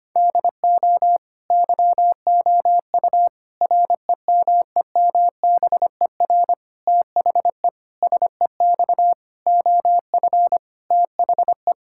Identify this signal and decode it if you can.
Morse code, heard as a keyed tone, about 25 wpm, 700 Hz. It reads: DO YOU REMEMBER THE SEX OF THE
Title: Sentences from Top 1600 Words 25wpm